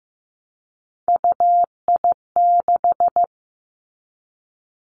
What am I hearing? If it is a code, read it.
Morse code, heard as a keyed tone, about 15 wpm, 700 Hz. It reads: UI6
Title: Three Challenging Letter-Number Combinations 15wpm